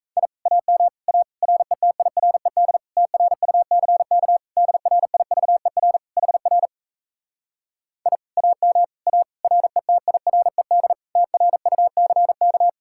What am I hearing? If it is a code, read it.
Morse code, heard as a keyed tone, about 38 wpm, 700 Hz. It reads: I AM A RETIRED TRUCK DRIVER HR I AM A RETIRED TRUCK